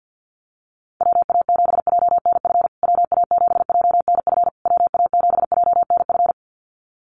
Morse code, heard as a keyed tone, about 50 wpm, 700 Hz. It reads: WA7JNF WA7JNF WA7JNF
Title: USA Call Signs Repeated 3x 50wpm